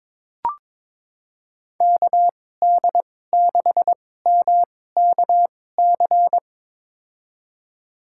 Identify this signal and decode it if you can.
Morse code, heard as a keyed tone, about 22 wpm, spaced as for 16 wpm, 700 Hz. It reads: KD6MKC
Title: USA Call Signs - 6 Characters - 2x Character Spacing - 22wpm